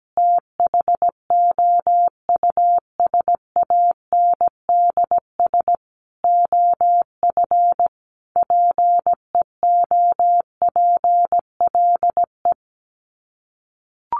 Morse code, heard as a keyed tone, about 17 wpm, 700 Hz. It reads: THOUSANDS OF PEOPLE
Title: Sets of 5 Words Encore2 17wpm